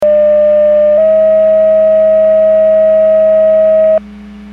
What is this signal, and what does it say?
Two-tone page, heard as a one-second tone, then a three-second tone, about 600 then 640 Hz. It is heard from 0.0 to 4.0 s.
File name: Portage / 00342 10-46 Portage 154.385 Paging